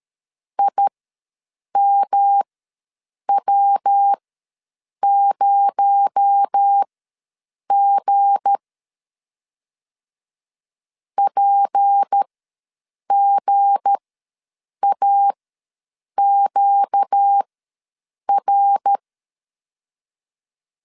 Morse code, heard as a keyed tone, about 13 wpm, spaced as for 7 wpm, 800 Hz. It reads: IMW0G PGAQR
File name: LEZIONI / 7 GF9Z / 7 G ws 12wpm 3140